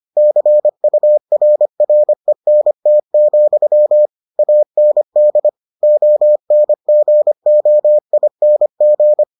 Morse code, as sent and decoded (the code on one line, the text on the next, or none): -.-. ..- .-. .-. -. - --..-- / .- -. -.. / --- -. --. --- .. -. --.
CURRENT, AND ONGOING